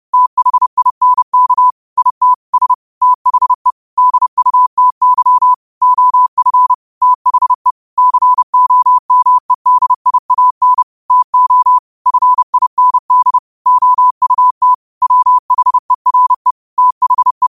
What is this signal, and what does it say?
Morse code, heard as a keyed tone, about 30 wpm, 1000 Hz. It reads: THINK IT S THE DUTY OF THE COMEDIAN TO FIND OUT WHERE THE